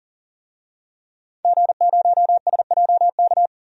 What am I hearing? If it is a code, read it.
Morse code, heard as a keyed tone, about 40 wpm, 700 Hz. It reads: G0SJK
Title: DXCC-100 Call Signs - 5 Characters - 40wpm